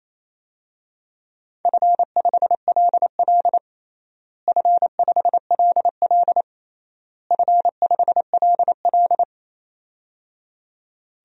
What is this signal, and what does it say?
Morse code, heard as a keyed tone, about 28 wpm, 700 Hz. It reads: F5LL F5LL F5LL
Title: DXCC-100 Call Signs - 4 Characters - Repeated 3x - 28wpm